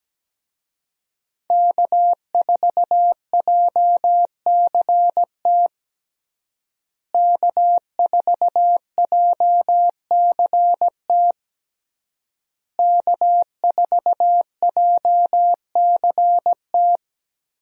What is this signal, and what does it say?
Morse code, heard as a keyed tone, about 17 wpm, 700 Hz. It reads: K4JCT K4JCT K4JCT
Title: USA Call Signs - 5 Characters - Repeated 3x - 17wpm